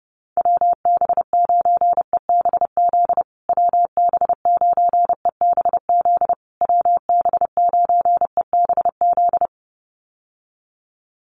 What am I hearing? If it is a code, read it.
Morse code, heard as a keyed tone, about 30 wpm, 700 Hz. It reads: WB9EBZ WB9EBZ WB9EBZ
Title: USA Call Signs Repeated 3x 30wpm